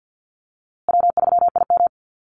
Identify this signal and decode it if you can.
Morse code, heard as a keyed tone, about 50 wpm, 700 Hz. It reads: W3IM
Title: USA Call Signs Repeated 2x 50wpm